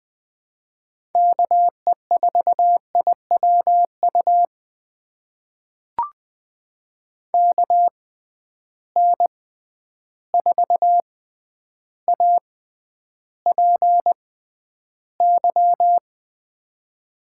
Morse code, heard as a keyed tone, about 20 wpm, spaced as for 10 wpm, 700 Hz. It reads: KE4IWU KN4APY
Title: USA Call Signs - 6 Characters - 6x Character Spacing - 20wpm